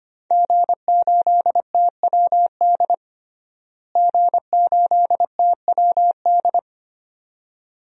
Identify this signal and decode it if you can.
Morse code, heard as a keyed tone, about 25 wpm, 700 Hz. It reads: G8TWD G8TWD
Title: DXCC-100 Call Signs - 5 Characters - Repeated 2x - 25wpm